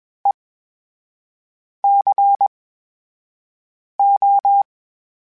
Morse code, heard as a keyed tone, about 21 wpm, spaced as for 5 wpm, 800 Hz. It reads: ECO
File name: Special INGs-mp3s / Special INGs-mp3s / VerySlowToFaster Ing @20